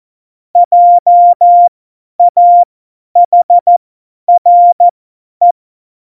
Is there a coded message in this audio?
Yes: Morse code from 0.6 to 5.5 s, heard as a keyed tone, about 14 wpm, spaced as for 10 wpm, 700 Hz. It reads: JAHRE